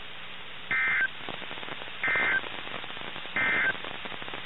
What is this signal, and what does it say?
Emergency Alert System (SAME) tones, heard as harsh buzzing bursts. It is heard from 0.6 to 3.8 s.
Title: NOAA Weather Radio - KIH61